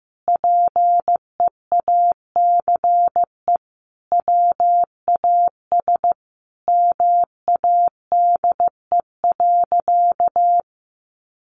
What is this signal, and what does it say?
Morse code, heard as a keyed tone, about 15 wpm, 700 Hz. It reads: PEACE WAS MADE.
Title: Sentences from Top 1500 Words 15wpm